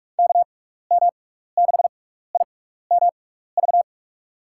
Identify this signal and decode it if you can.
Morse code, heard as a keyed tone, about 44 wpm, spaced as for 15 wpm, 700 Hz. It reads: KM6IMV
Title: USA Call Signs - 6 Characters - 6x Character Spacing - 45wpm